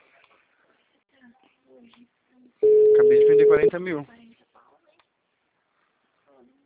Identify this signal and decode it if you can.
Ringback tone (Europe, 425 Hz), with speech.